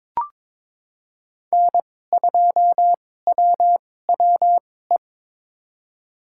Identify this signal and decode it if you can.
Morse code, heard as a keyed tone, about 22 wpm, spaced as for 16 wpm, 700 Hz. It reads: N2WWE
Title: USA Call Signs - 2x Character Spacing - 22wpm